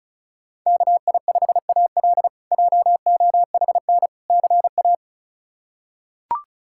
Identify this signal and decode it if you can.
Morse code, heard as a keyed tone, about 34 wpm, 700 Hz. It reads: KI5AL JOHN CA
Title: K1USN SST QSO Practice - 35wpm